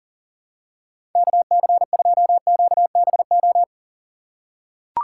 Morse code, heard as a keyed tone, about 40 wpm, 700 Hz. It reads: KC2QBO
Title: USA Call Signs - 2x Character Spacing - 40wpm